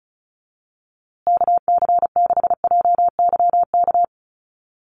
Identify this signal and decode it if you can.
Morse code, heard as a keyed tone, about 34 wpm, 700 Hz. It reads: KC6JYK
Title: USA Call Signs Repeated 3x 35wpm